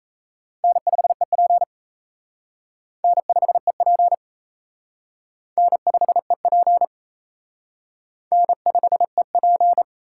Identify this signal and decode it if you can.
Morse code, heard as a keyed tone, about 34 wpm, 700 Hz. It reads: N5EP N5EP N5EP N5EP
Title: Speed Racing - USA 1x2 Call Signs 28wpm